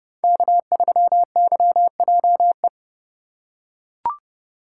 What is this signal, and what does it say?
Morse code, heard as a keyed tone, about 30 wpm, 700 Hz. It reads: K3YJE
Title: USA Call Signs - 4x Character Spacing - 30wpm